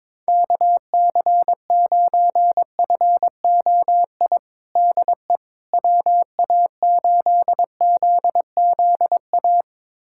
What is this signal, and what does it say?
Morse code, heard as a keyed tone, about 21 wpm, 700 Hz. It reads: KC9FOI DE WA8ZZA
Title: USA Callsign DE Callsign - 6 Characters - 22wpm